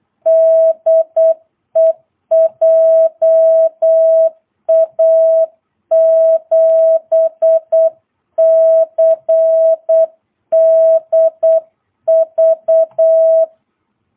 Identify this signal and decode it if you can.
Morse code, heard as a keyed tone, about 8 wpm, 660 Hz. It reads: DE JA7CDV
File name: jh8cbh / 2022cw